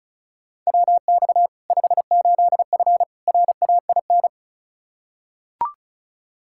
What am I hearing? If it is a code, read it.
Morse code, heard as a keyed tone, about 36 wpm, 700 Hz. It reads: WX 58F RAIN
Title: Practice QSO - Exchange 2 - Info - 35wpm